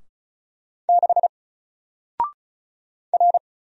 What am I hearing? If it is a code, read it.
Morse code, heard as a keyed tone, about 36 wpm, 700 Hz. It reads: 6 R
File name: Single-Letter-Number-35wpm